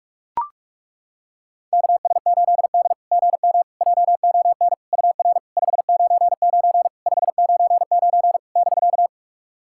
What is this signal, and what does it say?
Morse code, heard as a keyed tone, about 42 wpm, 690 Hz. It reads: KS8D GM JON UR 599 599 <BK>
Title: SOTA QSO Practice 45wpm